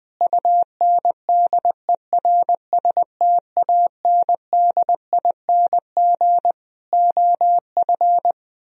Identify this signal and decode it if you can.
Morse code, heard as a keyed tone, about 21 wpm, 700 Hz. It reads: UNDERSTANDING OF
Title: Sentences from Top 1000 Words - Review Part 3 - 20wpm